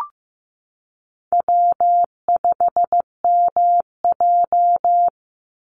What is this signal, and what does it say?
Morse code, heard as a keyed tone, about 15 wpm, 700 Hz. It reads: W5MJ
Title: Easy Call Signs Repeated 2x - 15wpm